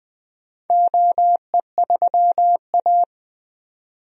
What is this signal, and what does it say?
Morse code, heard as a keyed tone, about 20 wpm, 700 Hz. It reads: OE3A
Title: DXCC-100 Call Signs - 4 Characters - 20wpm